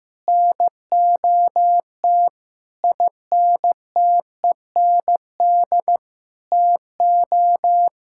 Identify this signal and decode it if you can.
Morse code, heard as a keyed tone, about 16 wpm, 700 Hz. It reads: NOT INTEND TO
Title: Sentences from Top 1600 Words 15wpm